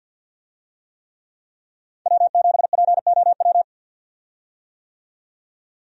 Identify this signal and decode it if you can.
Morse code, heard as a keyed tone, about 52 wpm, 700 Hz. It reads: W7POW
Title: USA Call Signs - 5 Characters - 50wpm